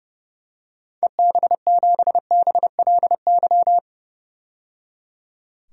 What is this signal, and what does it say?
Morse code, heard as a keyed tone, about 30 wpm, 700 Hz. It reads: EB7BLY
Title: DXCC-100 Call Signs - 6 Characters - 30wpm